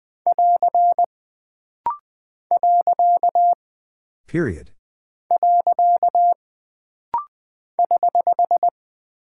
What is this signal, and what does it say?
Morse code, heard as a keyed tone, about 20 wpm, 700 Hz. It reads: <AR> . <HH>
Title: Single Punctuation-Prosign - Rapid-Fire - 20wpm